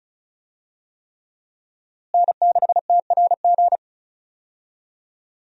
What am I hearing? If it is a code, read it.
Morse code, heard as a keyed tone, about 34 wpm, 700 Hz. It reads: N6TRG
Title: USA Call Signs - 5 Characters - Repeated 3x - 35wpm